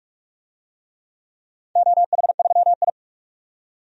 Morse code, heard as a keyed tone, about 46 wpm, 700 Hz. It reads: OH3I
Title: DXCC-100 Call Signs - 4 Characters - 45wpm